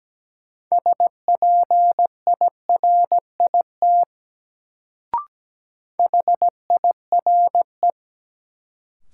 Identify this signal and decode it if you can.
Morse code, heard as a keyed tone, about 17 wpm, 700 Hz. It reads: SPIRIT HIRE
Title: Sentences from Top 1400 Words - New Word Drill - 17wpm